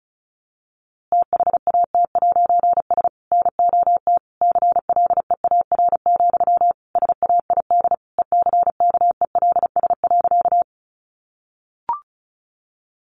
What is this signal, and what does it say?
Morse code, heard as a keyed tone, about 34 wpm, 700 Hz. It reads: THAT'S NOT CLEAR, SAID ECKELS.
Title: A Sound Of Thunder 35wpm